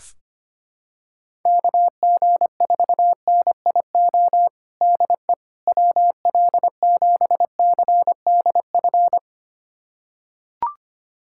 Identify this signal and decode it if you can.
Morse code, heard as a keyed tone, about 25 wpm, 700 Hz. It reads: KG4NIO DE WL7CDF
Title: USA Callsign DE Callsign - 6 Characters - 25wpm